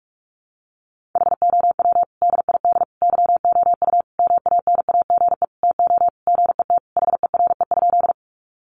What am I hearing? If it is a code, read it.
Morse code, heard as a keyed tone, about 44 wpm, 700 Hz. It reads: HOW DID YOU MANAGE TO GET HERE?
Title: Sentences from Top 1400 Words 45wpm